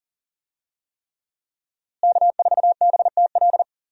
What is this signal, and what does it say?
Morse code, heard as a keyed tone, about 40 wpm, 700 Hz. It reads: K4BTL
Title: USA Call Signs - 5 Characters - Repeated 3x - 40wpm